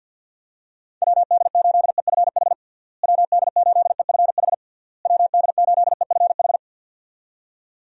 Morse code, heard as a keyed tone, about 52 wpm, 700 Hz. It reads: WD8EFH WD8EFH WD8EFH
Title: USA Call Signs - 6 Characters - Repeated 3x - 50wpm